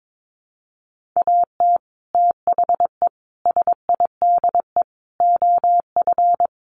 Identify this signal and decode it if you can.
Morse code, heard as a keyed tone, about 22 wpm, 700 Hz. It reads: AT THE SIDE OF
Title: Sets of 5 Words Encore2 22wpm